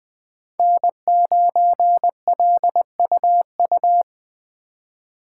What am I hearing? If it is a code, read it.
Morse code, heard as a keyed tone, about 20 wpm, 700 Hz. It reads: N9LUU
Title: USA Call Signs - 5 Characters - 2x Character Spacing - 20wpm